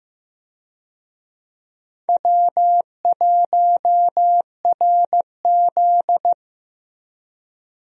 Morse code, heard as a keyed tone, about 15 wpm, 700 Hz. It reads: W1RZ
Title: USA Call Signs - 4 Characters - Repeated 2x - 15wpm